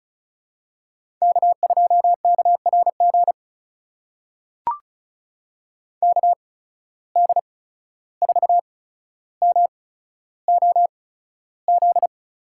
Morse code, heard as a keyed tone, about 36 wpm, spaced as for 14 wpm, 700 Hz. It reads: K2KRG KD4MOZ
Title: USA Call Signs - 8x Character Spacing - 35wpm